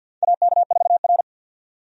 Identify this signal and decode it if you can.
Morse code, heard as a keyed tone, about 52 wpm, 700 Hz. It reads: AK4R
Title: USA Call Signs - 4 Characters - 50wpm